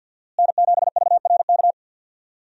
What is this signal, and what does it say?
Morse code, heard as a keyed tone, about 52 wpm, 700 Hz. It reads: N7VRK
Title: USA Call Signs - 5 Characters - Repeated 2x - 50wpm